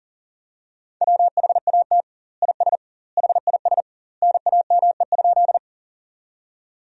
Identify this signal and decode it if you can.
Morse code, heard as a keyed tone, about 40 wpm, 700 Hz. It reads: WHAT IS HIS NAME?
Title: 2x Word-Spacing - Sentences from Top 200 Words - 40wpm